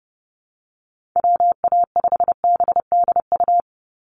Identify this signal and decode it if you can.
Morse code, heard as a keyed tone, about 30 wpm, 700 Hz. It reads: WA5BDU
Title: Call Signs - 30wpm